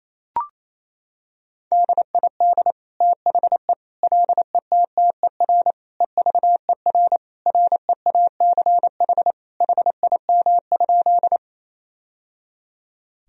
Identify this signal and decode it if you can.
Morse code, heard as a keyed tone, about 28 wpm, 700 Hz. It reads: DID THE LETTER EVER REACH HIM?
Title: Sentences from Top 1000 Words - Review Part 2 - 28wpm